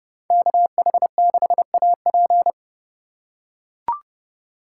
Morse code, heard as a keyed tone, about 29 wpm, 700 Hz. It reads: KH6AP